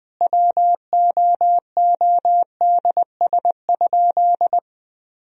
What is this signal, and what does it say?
Morse code, heard as a keyed tone, about 20 wpm, 700 Hz. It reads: WOODS?
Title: Sentences from Top 1000 Words - Review Part 2 - 20wpm